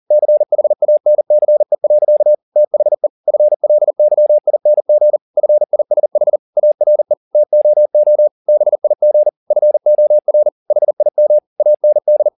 Morse code, heard as a keyed tone, about 40 wpm, 600 Hz. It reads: CHANCE. THE FLYING FISH ARE TOO BIG FOR HIM AND